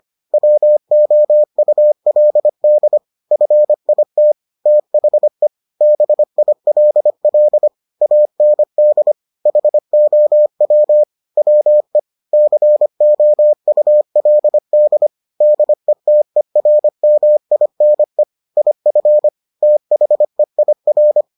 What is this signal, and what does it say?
Morse code, heard as a keyed tone, about 25 wpm, 600 Hz. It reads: WOULD FIT THE BILL AND HOW WE COULD DETERMINE IF THEIR